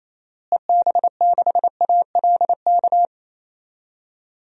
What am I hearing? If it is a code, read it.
Morse code, heard as a keyed tone, about 28 wpm, 700 Hz. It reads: EB6ALK
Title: DXCC-100 Call Signs - 6 Characters - Repeated 3x - 28wpm